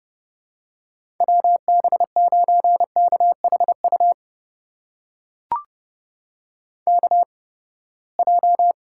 Morse code, heard as a keyed tone, about 31 wpm, spaced as for 21 wpm, 700 Hz. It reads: WB9KHU KJ
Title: USA Call Signs - 6 Characters - 8x Character Spacing - 30wpm